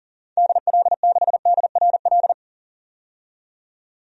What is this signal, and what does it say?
Morse code, heard as a keyed tone, about 39 wpm, 700 Hz. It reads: DL6DRL
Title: DXCC-100 Call Signs - 6 Characters - 40wpm